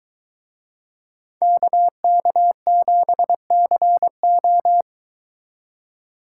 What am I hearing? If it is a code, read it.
Morse code, heard as a keyed tone, about 23 wpm, 700 Hz. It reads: KK7CO